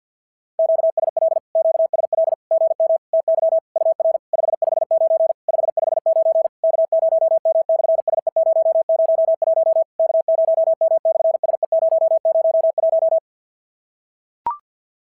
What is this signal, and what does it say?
Morse code, heard as a keyed tone, about 46 wpm, 650 Hz. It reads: QSL QSL GM TY UR 559 559 K0M/SE001 K0M/SE001
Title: SOTA QSO Practice 50wpm